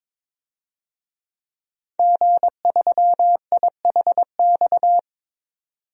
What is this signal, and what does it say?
Morse code, heard as a keyed tone, about 22 wpm, 700 Hz. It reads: G3IHX
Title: DXCC-100 Call Signs - 5 Characters - 22wpm